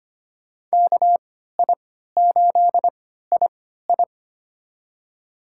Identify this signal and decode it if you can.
Morse code, heard as a keyed tone, about 25 wpm, spaced as for 14 wpm, 700 Hz. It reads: KI8II
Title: USA Call Signs - 3x Character Spacing - 25wpm